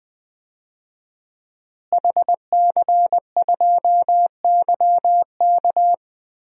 Morse code, heard as a keyed tone, about 20 wpm, 700 Hz. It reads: HC2YK
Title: DXCC-100 Call Signs - 5 Characters - Repeated 2x - 20wpm